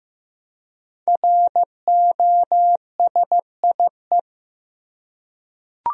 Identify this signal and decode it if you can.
Morse code, heard as a keyed tone, about 15 wpm, 700 Hz. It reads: ROSIE
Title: USA Common Names - Max 5 Letters - 15wpm